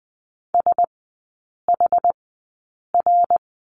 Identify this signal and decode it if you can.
Morse code, heard as a keyed tone, about 20 wpm, spaced as for 8 wpm, 700 Hz. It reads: SHR
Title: 114 Lesson 19e - Letter w - Three Characters 20wpm